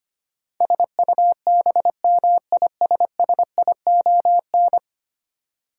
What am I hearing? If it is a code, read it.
Morse code, heard as a keyed tone, about 25 wpm, 700 Hz. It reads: SUBMISSION
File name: 10 Letter Words - First 3 Letters Given - 25wpm